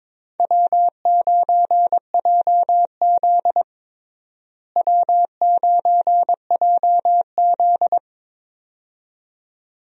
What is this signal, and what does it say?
Morse code, heard as a keyed tone, about 22 wpm, 700 Hz. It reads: W9JZ W9JZ
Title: USA Call Signs - 4 Characters - Repeated 2x - 22wpm